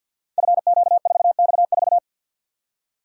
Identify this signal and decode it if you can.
Morse code, heard as a keyed tone, about 53 wpm, 700 Hz. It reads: FQ4X4
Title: Code Groups - 5 Letter:Numbers - 50wpm